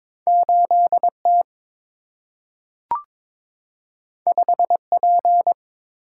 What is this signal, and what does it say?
Morse code, heard as a keyed tone, about 22 wpm, 690 Hz. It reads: 8T 5P
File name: practice / 22wpm / Two Letter-Number Combinations - 22wpm